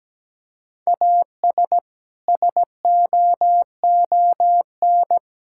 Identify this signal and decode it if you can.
Morse code, heard as a keyed tone, about 17 wpm, 700 Hz. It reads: AS SOON
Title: Sentences from Top 1600 Words 17wpm